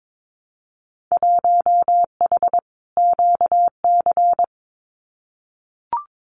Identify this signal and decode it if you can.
Morse code, heard as a keyed tone, about 22 wpm, 700 Hz. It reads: 1H QC
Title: Winter Field Day Exchanges 22wpm